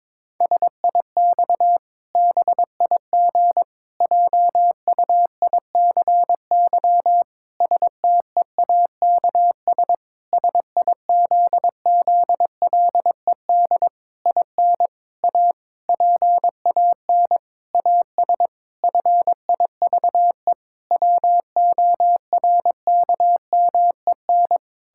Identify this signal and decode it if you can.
Morse code, heard as a keyed tone, about 22 wpm, 700 Hz. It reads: SIX BIG JUICY STEAKS SIZZLED IN A PAN AS FIVE WORKMEN
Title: Pangram Challenge v2 - 22wpm